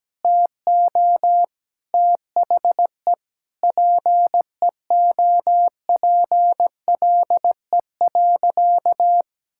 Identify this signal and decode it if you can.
Morse code, heard as a keyed tone, about 17 wpm, 700 Hz. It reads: TO THE PEOPLE.
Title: Sentences from Top 1000 Words - Review Part 6 - 17wpm